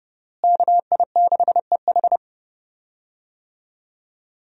Morse code, heard as a keyed tone, about 30 wpm, 700 Hz. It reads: KI6EH
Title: USA Call Signs - 5 Characters - Repeated 3x - 30wpm